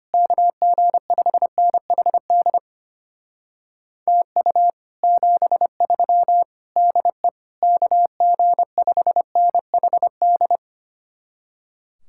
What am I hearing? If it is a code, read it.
Morse code, heard as a keyed tone, about 33 wpm, 700 Hz. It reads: KG5NHD TU 73 DE KG5NHD